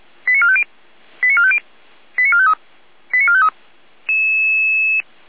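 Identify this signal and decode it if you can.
Five-tone selective call (ZVEI), heard as quick runs of five tones, tones of about 70 ms. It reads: 89450 89450 89453 89453
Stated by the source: ZVEI five-tone call received by SDR.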